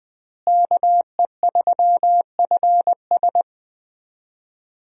Morse code, heard as a keyed tone, about 20 wpm, 700 Hz. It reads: KE3FS
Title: USA Call Signs - 5 Characters - Repeated 2x - 20wpm